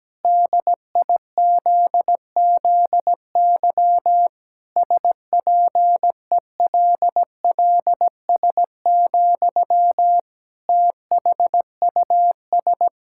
Morse code, heard as a keyed tone, about 17 wpm, 700 Hz. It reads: DIZZY SPELLS, THUS
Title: Pangram Challenge v2 - 17wpm